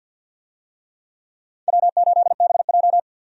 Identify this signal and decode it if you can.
Morse code, heard as a keyed tone, about 51 wpm, 700 Hz. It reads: W8BJ
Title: USA Call Signs - 4 Characters - 50wpm